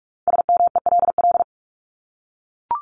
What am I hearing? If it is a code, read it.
Morse code, heard as a keyed tone, about 44 wpm, 700 Hz. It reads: SMELL
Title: Top 1501-1750 Words 45wpm